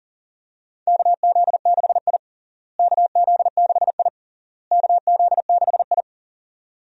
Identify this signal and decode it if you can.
Morse code, heard as a keyed tone, about 40 wpm, 700 Hz. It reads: KZ6I KZ6I KZ6I
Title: USA Call Signs - 4 Characters - Repeated 3x - 40wpm